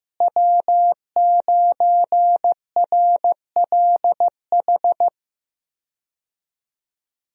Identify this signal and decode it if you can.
Morse code, heard as a keyed tone, about 15 wpm, 700 Hz. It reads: W9RLH
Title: USA Call Signs - 5 Characters - Repeated 2x - 15wpm